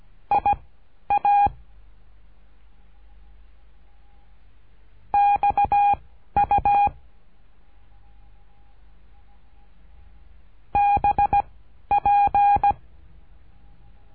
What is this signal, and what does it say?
Morse code, heard as a keyed tone, about 17 wpm, 810 Hz. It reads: I A X U B P
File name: cursos / CURSO DE TELEGRAFIA / Telegrafia - Trilha 13